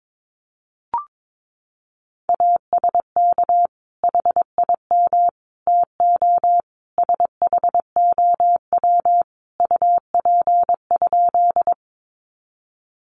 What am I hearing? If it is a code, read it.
Morse code, heard as a keyed tone, about 22 wpm, 700 Hz. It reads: ASK HIM TO SHOW UP?
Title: Sentences from Top 200 Words 22wpm